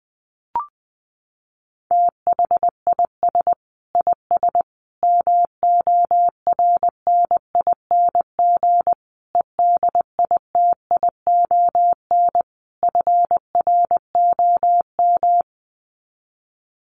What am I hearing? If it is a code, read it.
Morse code, heard as a keyed tone, about 20 wpm, 700 Hz. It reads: THIS IS MORNING EDITION FROM
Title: Sets of 5 Words Encore2 20wpm